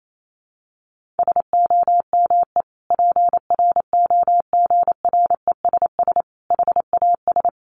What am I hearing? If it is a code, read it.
Morse code, heard as a keyed tone, about 28 wpm, 700 Hz. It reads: SOME PROGRESS HAS
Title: Sentences from Top 1400 Words 28wpm